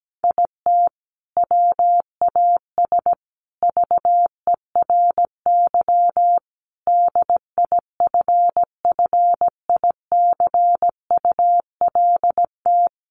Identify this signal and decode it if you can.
Morse code, heard as a keyed tone, about 17 wpm, 700 Hz. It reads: IT WAS VERY DIFFICULT
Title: Sentences from Top 1100 Words 17wpm